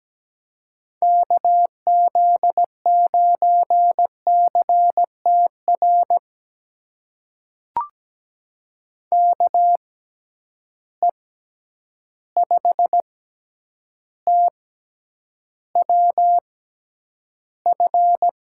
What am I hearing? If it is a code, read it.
Morse code, heard as a keyed tone, about 17 wpm, spaced as for 9 wpm, 700 Hz. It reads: KZ9CTR KE5TWF
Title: USA Call Signs - 6 Characters - 6x Character Spacing - 17wpm